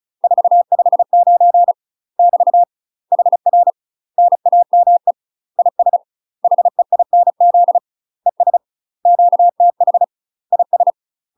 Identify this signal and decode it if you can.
Morse code, heard as a keyed tone, about 34 wpm, 700 Hz. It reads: 459 <BT> HR NAME IS HEINZ ES QTH IS
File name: UK MASTER QSOs TRAINING / 35WPM / UK MASTER QSOs 0005